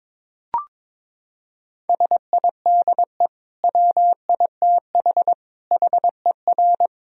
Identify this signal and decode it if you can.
Morse code, heard as a keyed tone, about 22 wpm, 700 Hz. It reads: SIDE WITH HER
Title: Sentences from Top 1000 Words - Review Part 1 - 22wpm